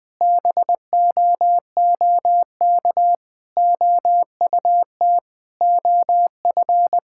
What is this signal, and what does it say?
Morse code, heard as a keyed tone, about 21 wpm, 700 Hz. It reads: BOOK OUT OF
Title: Sentences with 3-Word Phrases - 20wpm